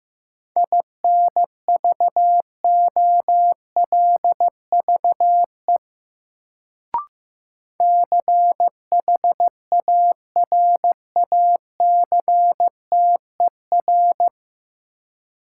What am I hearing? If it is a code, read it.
Morse code, heard as a keyed tone, about 15 wpm, 700 Hz. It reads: INVOLVE CHARACTER
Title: Sentences from Top 1200 Words - New Word Drill - 15wpm